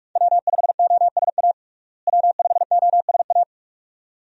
Morse code, heard as a keyed tone, about 43 wpm, 700 Hz. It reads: W5OSA W5OSA
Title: USA Call Signs - 5 Characters - Repeated 2x - 45wpm